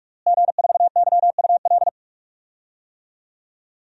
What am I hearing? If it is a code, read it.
Morse code, heard as a keyed tone, about 46 wpm, 700 Hz. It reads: G4YUL
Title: DXCC-100 Call Signs - 5 Characters - Repeated 3x - 45wpm